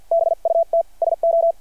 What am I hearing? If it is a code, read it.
Morse code, heard as a keyed tone, about 53 wpm, spaced as for 44 wpm, 660 Hz. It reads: BUT SO
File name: archived / CRSnet